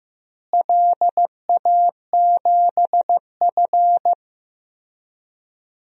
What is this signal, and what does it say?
Morse code, heard as a keyed tone, about 15 wpm, 700 Hz. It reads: LA7F
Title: DXCC-100 Call Signs - 4 Characters - Repeated 3x - 15wpm